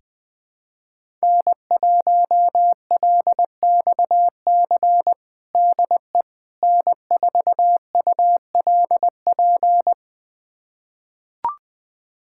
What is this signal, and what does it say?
Morse code, heard as a keyed tone, about 21 wpm, 700 Hz. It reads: N1LXC DE N4ULP
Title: USA Callsign DE Callsign - 5 Characters - 20wpm